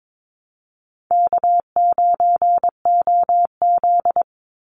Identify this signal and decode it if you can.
Morse code, heard as a keyed tone, about 22 wpm, 700 Hz. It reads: K9OZ
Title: NA SOTA Chaser Callsigns 22wpm